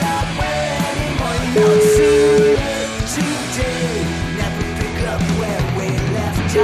Ringback tone (Europe, 425 Hz), with noise.